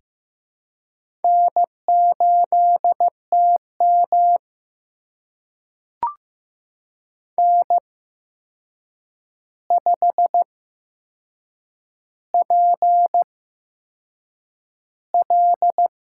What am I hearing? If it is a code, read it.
Morse code, heard as a keyed tone, about 15 wpm, spaced as for 6 wpm, 700 Hz. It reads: N8TM N5PL
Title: USA Call Signs - 8x Character Spacing - 15wpm